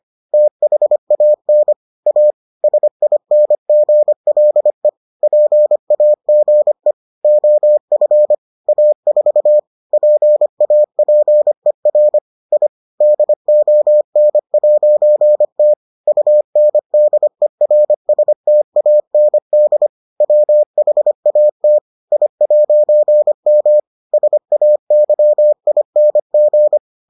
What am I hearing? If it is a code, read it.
Morse code, heard as a keyed tone, about 25 wpm, 600 Hz. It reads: THAN A SINGLE PAGE OF A4 PAPER I DON'T UNDERSTAND WHAT I'M SAYING